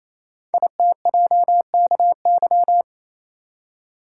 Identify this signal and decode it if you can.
Morse code, heard as a keyed tone, about 29 wpm, 700 Hz. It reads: ITJKY
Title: Code Groups - 5 Letters - 28wpm